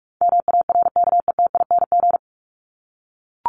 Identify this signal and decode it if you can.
Morse code, heard as a keyed tone, about 44 wpm, 700 Hz. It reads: MARKETING
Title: ING Suffix - Top 100 Words with Base Words 45wpm